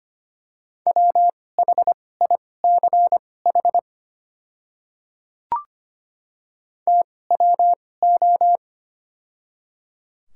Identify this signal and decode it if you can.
Morse code, heard as a keyed tone, about 25 wpm, spaced as for 18 wpm, 700 Hz. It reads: WHICH TWO
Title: 2x Character Spacing - Top 100 Words - 25wpm